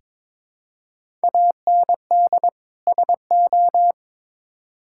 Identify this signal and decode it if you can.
Morse code, heard as a keyed tone, about 22 wpm, 700 Hz. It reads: AND SO
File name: Build A Phrase - 2-Word Phrases - 22wpm